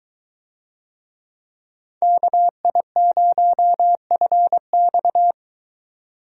Morse code, heard as a keyed tone, about 21 wpm, 700 Hz. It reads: KI0FX